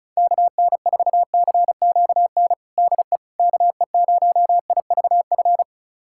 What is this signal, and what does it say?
Morse code, heard as a keyed tone, about 36 wpm, 700 Hz. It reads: KN4CQN DE KE0IVF
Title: USA Callsign DE Callsign - 6 Characters - 35wpm